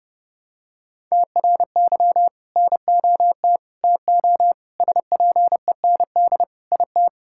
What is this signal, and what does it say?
Morse code, heard as a keyed tone, about 30 wpm, 700 Hz. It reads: TRY NOT TO SPEND IT
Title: Sentences from Top 1000 Words - Review Part 5 - 30wpm